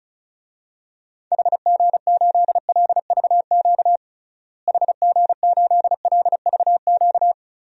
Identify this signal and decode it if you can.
Morse code, heard as a keyed tone, about 36 wpm, 700 Hz. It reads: HG8LVQ HG8LVQ
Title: DXCC-100 Call Signs - 6 Characters - Repeated 3x - 35wpm